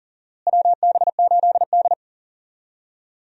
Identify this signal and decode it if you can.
Morse code, heard as a keyed tone, about 37 wpm, 700 Hz. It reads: WB8D
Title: USA Call Signs - 4 Characters - Repeated 3x - 40wpm